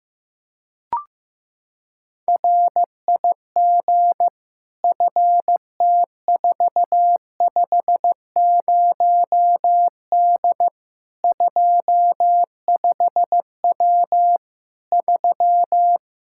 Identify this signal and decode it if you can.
Morse code, heard as a keyed tone, about 15 wpm, 700 Hz. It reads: RIG FT450D 25W 3
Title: Practice QSO - Exchange 2 - Info - 15wpm